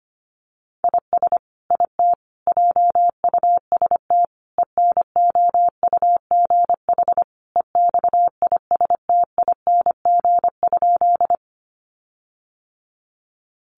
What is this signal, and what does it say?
Morse code, heard as a keyed tone, about 25 wpm, 700 Hz. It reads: IS IT JUST ENOUGH EXISTING?
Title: Sentences from Top 1500 Words 25wpm